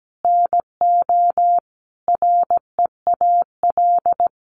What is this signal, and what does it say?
Morse code, heard as a keyed tone, about 17 wpm, 700 Hz. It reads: NO REAL
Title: Sentences from Top 1300 Words 17wpm